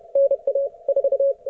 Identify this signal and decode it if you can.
Morse code, heard as a keyed tone, about 31 wpm, 550 Hz. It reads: NA 4
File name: Archiv / CUC138 / ok1fhi cul138